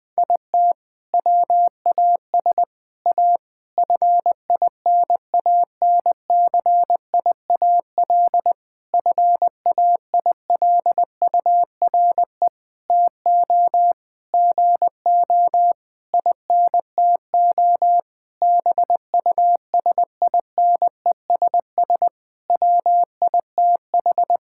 Morse code, heard as a keyed tone, about 20 wpm, 700 Hz. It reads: IT WAS A FINANCIAL FAILURE TO GO INTO BUSINESS WITH
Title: Sentences from Top 1000 Words - Review Part 6 - 20wpm